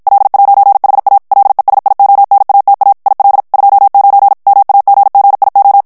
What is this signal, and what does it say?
Morse code, heard as a keyed tone, about 56 wpm, 800 Hz. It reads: L'HA LESIONATA IL 29 MAGGIO
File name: morse mp3 / ITALIEN texte clair 1 vitesse 50 mots minute